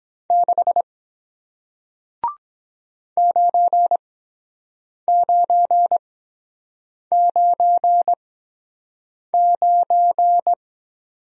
Morse code, heard as a keyed tone, about 23 wpm, spaced as for 7 wpm, 700 Hz. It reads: 6 9999